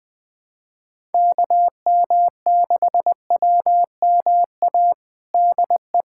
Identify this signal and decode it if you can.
Morse code, heard as a keyed tone, about 20 wpm, 700 Hz. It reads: KM6WMA DE
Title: USA Callsign DE Callsign - 6 Characters - 20wpm